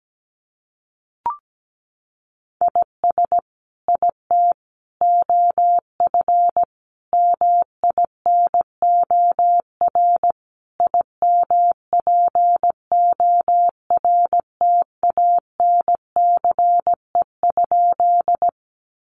Morse code, heard as a keyed tone, about 17 wpm, 700 Hz. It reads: IS IT OF MINOR IMPORTANCE?
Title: Sentences from Top 1500 Words 17wpm